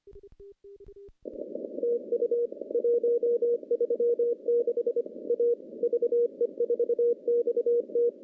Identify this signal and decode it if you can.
Morse code, heard as a keyed tone, about 25 wpm, 480 Hz. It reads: TU 136 A VE4XT
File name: CW-SS-k5tr